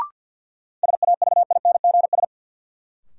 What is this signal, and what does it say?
Morse code, heard as a keyed tone, about 53 wpm, 700 Hz. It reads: SAVINGS
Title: Sentences from Top 1200 Words - New Word Drill Alt - 50wpm